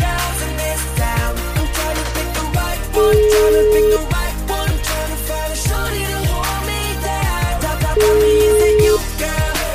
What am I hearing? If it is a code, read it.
Ringback tone (Europe, 425 Hz), with noise.